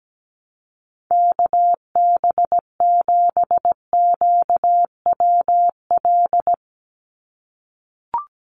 Morse code, heard as a keyed tone, about 17 wpm, 700 Hz. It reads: KB7QWL
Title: USA Call Signs 17wpm